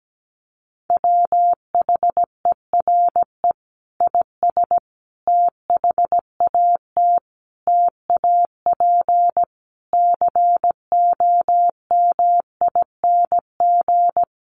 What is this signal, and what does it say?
Morse code, heard as a keyed tone, about 17 wpm, 700 Hz. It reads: WHERE IS THAT TAP COMING
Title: Sentences from Top 1400 Words 17wpm